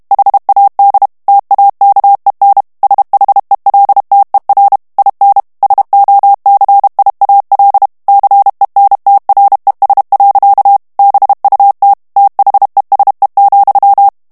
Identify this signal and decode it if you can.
Morse code, heard as a keyed tone, about 32 wpm, 800 Hz. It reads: HAD TAKEN SHELTER IN SOCIAL CENTRES. BUT THESE,
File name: ANGLAIS texte clair 2 vitesse 30 mots minute